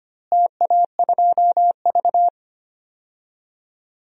Morse code, heard as a keyed tone, about 25 wpm, 700 Hz. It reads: TA2V
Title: DXCC-100 Call Signs - 4 Characters - Repeated 2x - 25wpm